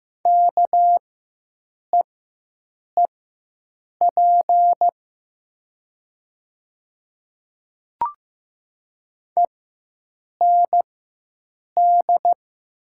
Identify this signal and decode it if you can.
Morse code, heard as a keyed tone, about 15 wpm, spaced as for 7 wpm, 700 Hz. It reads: KEEP END
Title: 4x Character Spacing - Top 300 Words - 15wpm